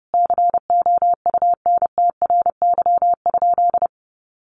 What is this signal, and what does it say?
Morse code, heard as a keyed tone, about 28 wpm, 700 Hz. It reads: COUNTRY?
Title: Sentences from Top 1400 Words 30wpm